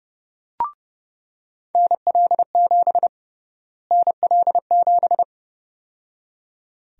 Morse code, heard as a keyed tone, about 30 wpm, 700 Hz. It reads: NL7 NL7
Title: USA Call Sign Prefixes - Repeated 2x - 30wpm